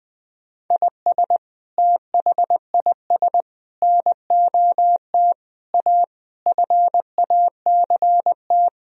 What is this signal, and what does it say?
Morse code, heard as a keyed tone, about 21 wpm, 700 Hz. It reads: IS THIS NOT A FACT
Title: Sentences from Top 1000 Words - Review Part 4 - 20wpm